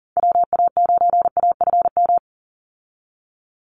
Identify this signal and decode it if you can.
Morse code, heard as a keyed tone, about 40 wpm, 700 Hz. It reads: WA9AFM
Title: USA Call Signs 40wpm